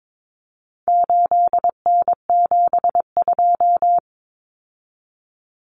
Morse code, heard as a keyed tone, about 22 wpm, 700 Hz. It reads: 8N72
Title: Common Cut Numbers - 4-Digit 22wpm